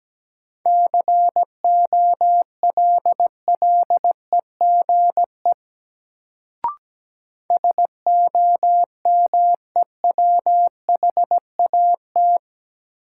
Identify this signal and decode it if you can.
Morse code, heard as a keyed tone, about 17 wpm, 700 Hz. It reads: COLLEGE SOMEWHAT
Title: Sentences from Top 1300 Words - New Word Drill Alt - 17wpm